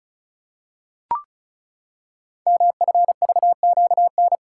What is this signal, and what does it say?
Morse code, heard as a keyed tone, about 36 wpm, 700 Hz. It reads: MFVQN
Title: Code Groups - 5 Letters - 35wpm